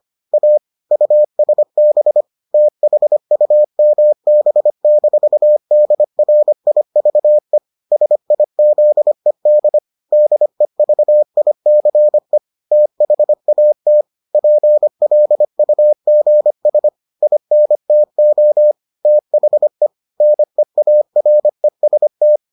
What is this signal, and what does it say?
Morse code, heard as a keyed tone, about 25 wpm, 600 Hz. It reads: A USB THUMB-DRIVE SIZED DEVICE THAT PLUGS INTO THE NEAREST